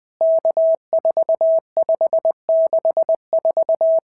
Morse code, heard as a keyed tone, about 20 wpm, 650 Hz. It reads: K4564
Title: POTA QSO Practice - 20wpm